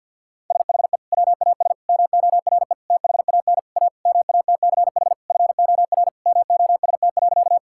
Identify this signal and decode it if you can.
Morse code, heard as a keyed tone, about 50 wpm, 700 Hz. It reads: SHE WAS MORE THAN A MATCH FOR MOST.
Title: Sentences from Top 1600 Words 50wpm